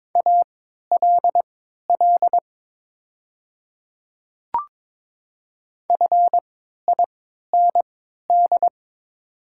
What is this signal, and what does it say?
Morse code, heard as a keyed tone, about 22 wpm, spaced as for 12 wpm, 700 Hz. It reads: ALL FIND
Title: 3x Character Spacing - Top 100 Words - 22wpm